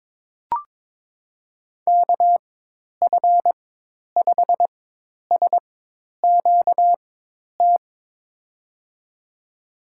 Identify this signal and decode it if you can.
Morse code, heard as a keyed tone, about 22 wpm, spaced as for 10 wpm, 700 Hz. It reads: KF5SQT
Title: USA Call Signs - 6 Characters - 4x Character Spacing - 22wpm